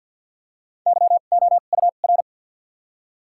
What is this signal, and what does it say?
Morse code, heard as a keyed tone, about 54 wpm, spaced as for 37 wpm, 700 Hz. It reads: YOUR
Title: 2x Character Spacing - Top 300 Words - 50wpm